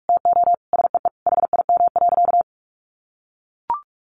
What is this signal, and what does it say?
Morse code, heard as a keyed tone, about 44 wpm, 700 Hz. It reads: TO SEE HIM.
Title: Sentences from Top 1500 Words 45wpm